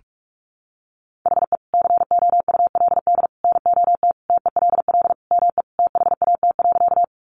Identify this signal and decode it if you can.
Morse code, heard as a keyed tone, about 44 wpm, 700 Hz. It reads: HE COULD NOT TELL ME THAT.
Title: Sentences from Top 600 Words 45wpm